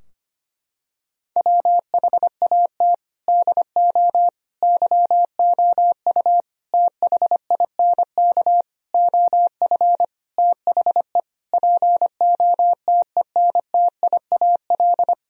Morse code, heard as a keyed tone, about 25 wpm, 700 Hz. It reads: WHAT DO YOU THINK OF THE POTENTIAL
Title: Sentences from Top 1000 Words - Review Part 4 - 25wpm